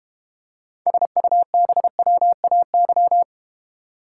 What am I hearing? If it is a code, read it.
Morse code, heard as a keyed tone, about 31 wpm, 700 Hz. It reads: SUBWAY
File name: Easy Words - Speed-Racing with Hints - 25wpm